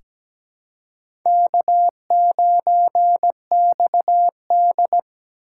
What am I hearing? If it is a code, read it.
Morse code, heard as a keyed tone, about 17 wpm, 700 Hz. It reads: K9XD
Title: K1USN SST QSO Practice - 17wpm